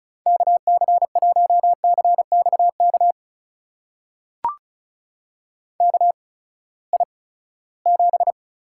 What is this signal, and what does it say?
Morse code, heard as a keyed tone, about 36 wpm, spaced as for 20 wpm, 700 Hz. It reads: KC1CXK KI7
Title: USA Call Signs - 6 Characters - 8x Character Spacing - 35wpm